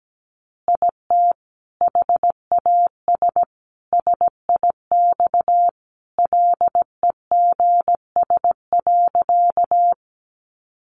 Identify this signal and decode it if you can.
Morse code, heard as a keyed tone, about 17 wpm, 700 Hz. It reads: IT HAS SIX LEGS.
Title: Additional Easy Phrases 17wpm